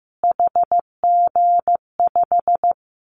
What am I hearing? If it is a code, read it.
Morse code, heard as a keyed tone, about 15 wpm, 700 Hz. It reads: HG5
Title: Callsign Prefixes for Top 25 DXCC - Repeated 3x - 15wpm